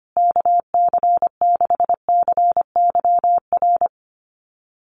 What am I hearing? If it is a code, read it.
Morse code, heard as a keyed tone, about 25 wpm, 700 Hz. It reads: KC6CYR
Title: USA Call Signs Repeated 2x 25wpm